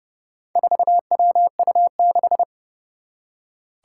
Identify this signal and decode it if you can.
Morse code, heard as a keyed tone, about 30 wpm, 700 Hz. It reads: 4WU6
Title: Four Challenging Letter-Number Combinations 30wpm